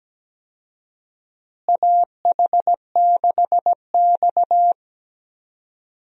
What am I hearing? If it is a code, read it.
Morse code, heard as a keyed tone, about 17 wpm, 700 Hz. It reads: AH6X
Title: USA Call Signs - 4 Characters - Repeated 2x - 17wpm